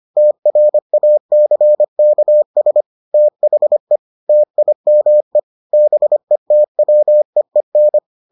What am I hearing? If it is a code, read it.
Morse code, heard as a keyed tone, about 25 wpm, 600 Hz. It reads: TRACKS THE TIME BETWEEN